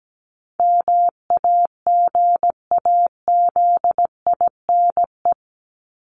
Morse code, heard as a keyed tone, about 17 wpm, 700 Hz. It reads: MAGAZINE
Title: Top 751-1000 Words 17wpm